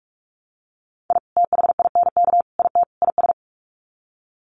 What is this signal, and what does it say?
Morse code, heard as a keyed tone, about 44 wpm, 700 Hz. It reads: I THINK IT IS
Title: Sets of 4 Words 45wpm